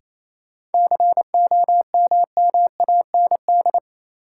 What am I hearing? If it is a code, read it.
Morse code, heard as a keyed tone, about 28 wpm, 700 Hz. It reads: COMMAND
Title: Sentences from Top 1600 Words - New Word Drill Alt - 28wpm